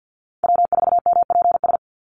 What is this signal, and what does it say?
Morse code, heard as a keyed tone, about 50 wpm, 700 Hz. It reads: W4MPS
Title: NA SOTA Chaser Callsigns 50wpm